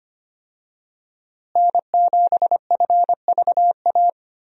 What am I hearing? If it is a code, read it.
Morse code, heard as a keyed tone, about 25 wpm, 700 Hz. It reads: N7FVA
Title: USA Call Signs - 5 Characters - Repeated 2x - 25wpm